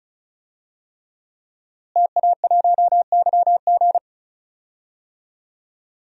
Morse code, heard as a keyed tone, about 36 wpm, 700 Hz. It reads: TA1YG
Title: DXCC-100 Call Signs - 5 Characters - Repeated 3x - 35wpm